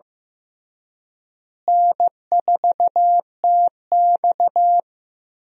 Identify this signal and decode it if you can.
Morse code, heard as a keyed tone, about 15 wpm, 700 Hz. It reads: N4TX
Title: USA Call Signs - 4 Characters - Repeated 3x - 15wpm